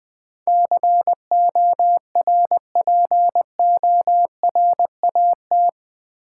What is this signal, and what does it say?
Morse code, heard as a keyed tone, about 20 wpm, 700 Hz. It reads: CORPORAT
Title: Build A Word - 4 Letters At A Time - 11-Letter Words - 20wpm